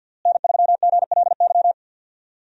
Morse code, heard as a keyed tone, about 50 wpm, 700 Hz. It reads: N3GLY
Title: USA Call Signs - 5 Characters - Repeated 3x - 50wpm